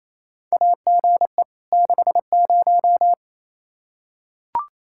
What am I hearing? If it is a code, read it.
Morse code, heard as a keyed tone, about 28 wpm, 700 Hz. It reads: AGE 60
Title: Practice QSO - Exchange 3 - More Info - 28wpm